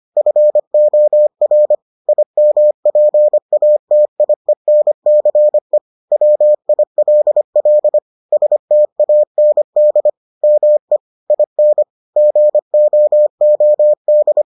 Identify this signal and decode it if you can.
Morse code, heard as a keyed tone, about 25 wpm, 600 Hz. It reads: FOR IMPATIENCE WILL STAND ME IN GOOD